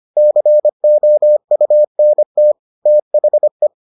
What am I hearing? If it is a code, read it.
Morse code, heard as a keyed tone, about 25 wpm, 600 Hz. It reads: COUNT THE